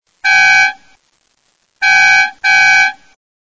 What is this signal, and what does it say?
There is no code.